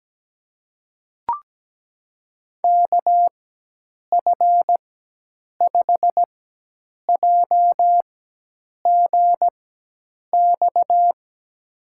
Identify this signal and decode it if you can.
Morse code, heard as a keyed tone, about 17 wpm, spaced as for 8 wpm, 700 Hz. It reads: KF5JGX
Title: USA Call Signs - 6 Characters - 4x Character Spacing - 17wpm